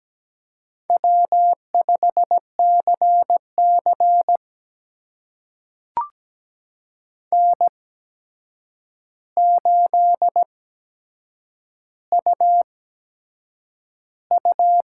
Morse code, heard as a keyed tone, about 17 wpm, spaced as for 7 wpm, 700 Hz. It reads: W5CC N8UU
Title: USA Call Signs - 8x Character Spacing - 17wpm